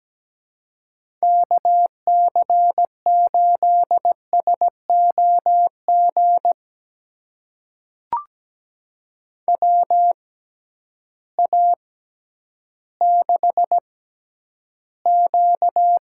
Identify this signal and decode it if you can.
Morse code, heard as a keyed tone, about 17 wpm, spaced as for 10 wpm, 700 Hz. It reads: KC8SOG WA6Q
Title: USA Call Signs - 6 Characters - 6x Character Spacing - 17wpm